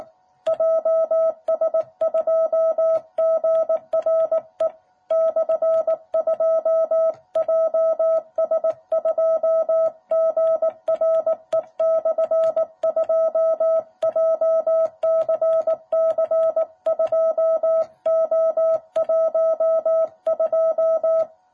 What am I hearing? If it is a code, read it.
Morse code, heard as a keyed tone, about 18 wpm, 660 Hz. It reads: JS2GRE/2 JS2GRE/2 JCC2O12
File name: jh8cbh / 2021cw